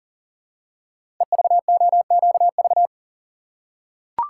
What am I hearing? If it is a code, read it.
Morse code, heard as a keyed tone, about 40 wpm, 700 Hz. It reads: EVOQV
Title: Code Groups - 5 Letters - 40wpm